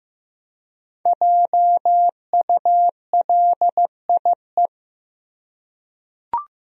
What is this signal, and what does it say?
Morse code, heard as a keyed tone, about 15 wpm, 700 Hz. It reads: JULIE
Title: USA Common Names - 15wpm